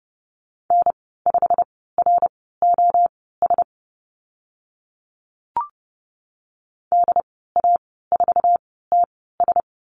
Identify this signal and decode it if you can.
Morse code, heard as a keyed tone, about 30 wpm, spaced as for 17 wpm, 700 Hz. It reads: N5ROS DA4TS
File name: Lesson 15g - Number 5 - Call Signs 30wpm